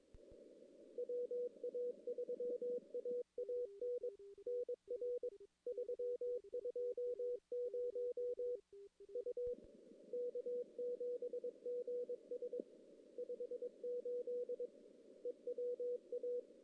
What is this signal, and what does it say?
Morse code, heard as a keyed tone, about 22 wpm, 490 Hz. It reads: WA3AAN NR 320 U K7GS 58 EWA
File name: CW / CW-SS-k5tr- -1700